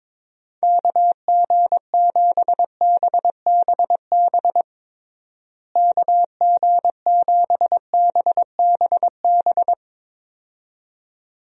Morse code, heard as a keyed tone, about 22 wpm, 700 Hz. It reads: KG7BBB KG7BBB
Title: USA Call Signs - 6 Characters - Repeated 2x - 22wpm